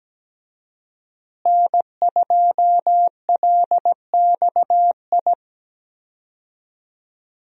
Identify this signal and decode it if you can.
Morse code, heard as a keyed tone, about 17 wpm, 700 Hz. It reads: N2LXI
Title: USA Call Signs - 5 Characters - Repeated 3x - 17wpm